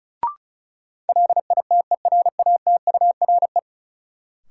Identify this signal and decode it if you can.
Morse code, heard as a keyed tone, about 36 wpm, 700 Hz. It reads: LITERATURE
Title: Sentences from Top 600 Words - New Word Drill - 35wpm